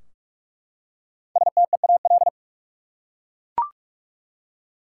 Morse code, heard as a keyed tone, about 43 wpm, 700 Hz. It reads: STEAL
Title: Five Letter Words - 45wpm